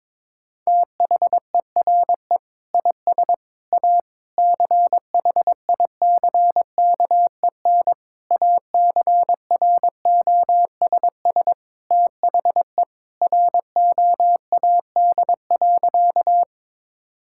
Morse code, heard as a keyed tone, about 22 wpm, 700 Hz. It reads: THERE IS A CHICKEN ACROSS THE ROAD.
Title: Sentences from Top 1000 Words - Review Part 5 - 22wpm